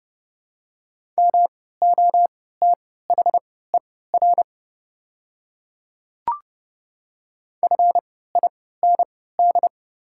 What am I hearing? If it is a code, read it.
Morse code, heard as a keyed tone, about 30 wpm, spaced as for 17 wpm, 690 Hz. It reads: MOTHER FIND
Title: 3x Character Spacing - Top 200 Words - 30wpm